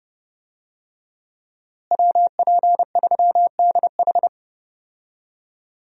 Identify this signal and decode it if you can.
Morse code, heard as a keyed tone, about 32 wpm, 700 Hz. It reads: WP3DH
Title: USA Call Signs - 5 Characters - Repeated 2x - 30wpm